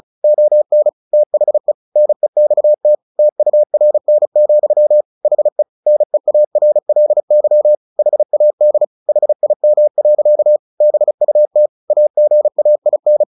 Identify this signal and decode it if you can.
Morse code, heard as a keyed tone, about 34 wpm, 600 Hz. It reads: ON THE NEXT TURN, HE NEARLY HAD HIM. BUT AGAIN
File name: the old man and the sea 2 35wpm